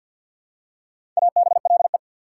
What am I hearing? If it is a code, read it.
Morse code, heard as a keyed tone, about 54 wpm, 690 Hz. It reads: ABLE
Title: Prefixes and Suffixes 50wpm